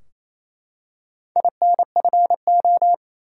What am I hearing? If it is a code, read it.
Morse code, heard as a keyed tone, about 29 wpm, 700 Hz. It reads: INFO
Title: Top 180 QSO Elements 28wpm